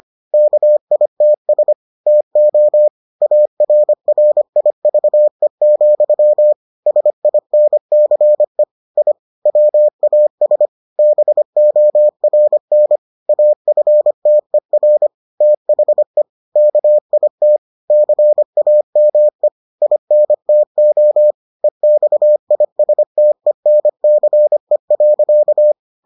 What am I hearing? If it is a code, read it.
Morse code, heard as a keyed tone, about 25 wpm, 600 Hz. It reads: KITS TO ARRIVE, SINCE I WAS BORN AFTER THE KIT CAME INTO EXISTENCE.